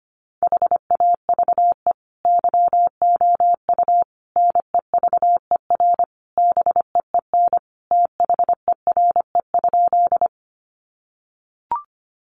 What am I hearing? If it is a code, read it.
Morse code, heard as a keyed tone, about 25 wpm, 700 Hz. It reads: HAVE YOU NEVER BEEN THERE?
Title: Sentences from Top 300 Words 25wpm